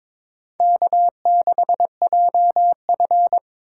Morse code, heard as a keyed tone, about 23 wpm, 700 Hz. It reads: K6JF
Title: USA Call Signs - 4 Characters - 2x Character Spacing - 22wpm